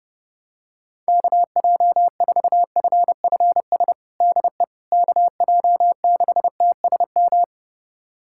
Morse code, heard as a keyed tone, about 30 wpm, 700 Hz. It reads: KJ4FFS DE KJ6TSM
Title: USA Callsign DE Callsign - 6 Characters - 30wpm